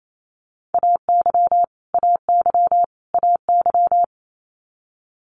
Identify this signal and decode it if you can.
Morse code, heard as a keyed tone, about 28 wpm, 700 Hz. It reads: AY AY AY
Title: Multi-letter Phonemes Repeated 3x 28wpm